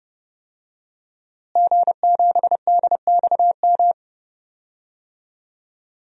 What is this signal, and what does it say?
Morse code, heard as a keyed tone, about 30 wpm, 700 Hz. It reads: G7DXM
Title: DXCC-100 Call Signs - 5 Characters - Repeated 2x - 30wpm